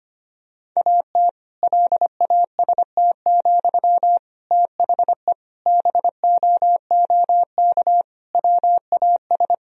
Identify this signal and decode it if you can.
Morse code, heard as a keyed tone, about 25 wpm, 700 Hz. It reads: AT LAST, THE BOOK WAS
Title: Sentences from Top 1000 Words - Review Part 3 - 25wpm